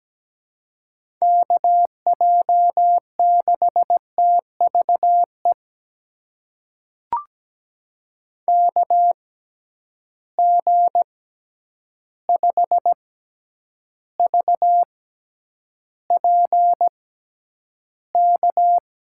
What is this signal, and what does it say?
Morse code, heard as a keyed tone, about 17 wpm, spaced as for 9 wpm, 700 Hz. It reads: KJ6TVE KG5VPK
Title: USA Call Signs - 6 Characters - 6x Character Spacing - 17wpm